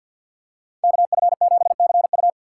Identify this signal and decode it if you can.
Morse code, heard as a keyed tone, about 52 wpm, 700 Hz. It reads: KF7CU
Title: USA Call Signs - 5 Characters - 3x Character Spacing - 50wpm